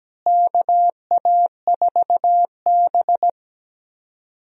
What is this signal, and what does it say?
Morse code, heard as a keyed tone, about 17 wpm, 700 Hz. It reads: KA4B
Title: USA Call Signs - 4 Characters - Repeated 3x - 17wpm